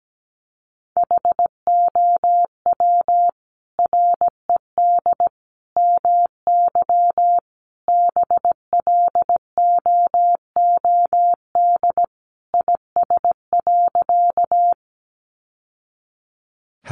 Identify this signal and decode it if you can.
Morse code, heard as a keyed tone, about 17 wpm, 700 Hz. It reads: HOW RED MY BLOOD IS.
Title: Sentences from Top 1100 Words 17wpm